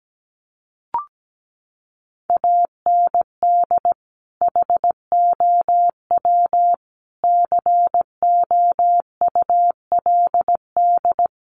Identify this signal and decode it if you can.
Morse code, heard as a keyed tone, about 17 wpm, 700 Hz. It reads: AND HOW COULD
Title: Sentences from Top 1500 Words 17wpm